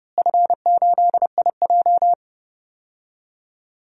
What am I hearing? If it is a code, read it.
Morse code, heard as a keyed tone, about 30 wpm, 700 Hz. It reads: F8IJ
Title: DXCC-100 Call Signs - 4 Characters - Repeated 3x - 30wpm